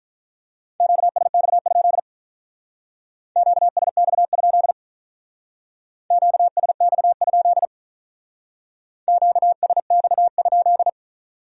Morse code, heard as a keyed tone, about 44 wpm, 700 Hz. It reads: QSX? QSX? QSX? QSX?